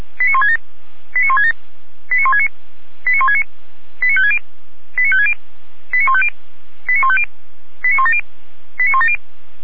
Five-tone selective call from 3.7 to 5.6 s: 89560 89560

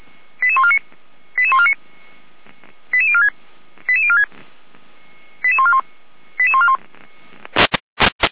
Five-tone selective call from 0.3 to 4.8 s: 88149 88149 88046 88046

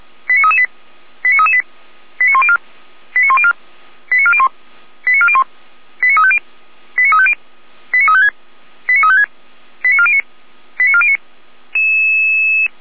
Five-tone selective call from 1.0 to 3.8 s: 89338 89104 89104

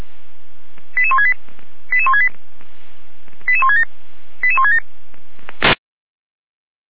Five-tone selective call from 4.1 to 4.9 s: 88167

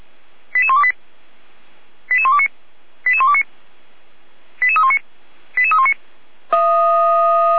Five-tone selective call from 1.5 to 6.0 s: 88129 88129 88319 88319

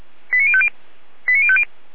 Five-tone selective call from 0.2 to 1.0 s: 89050

